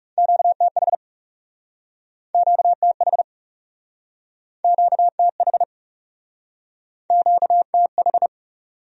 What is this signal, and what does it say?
Morse code, heard as a keyed tone, about 40 wpm, 700 Hz. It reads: QTH QTH QTH QTH